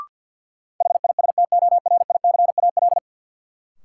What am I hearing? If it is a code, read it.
Morse code, heard as a keyed tone, about 53 wpm, 700 Hz. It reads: HISTORICAL
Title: Sentences from Top 600 Words - New Word Drill - 50wpm